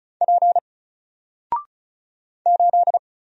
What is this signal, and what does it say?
Morse code, heard as a keyed tone, about 36 wpm, 700 Hz. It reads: P 8
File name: Single-Letter-Number-35wpm